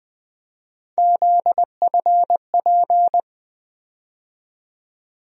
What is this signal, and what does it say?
Morse code, heard as a keyed tone, about 20 wpm, 700 Hz. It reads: ZFP
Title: USA Call Sign Postfixes - 3 Letters 20wpm